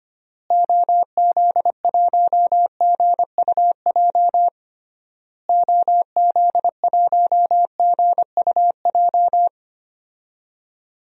Morse code, heard as a keyed tone, about 25 wpm, 700 Hz. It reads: OZ1GUJ OZ1GUJ
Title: DXCC-100 Call Signs - 6 Characters - Repeated 3x - 25wpm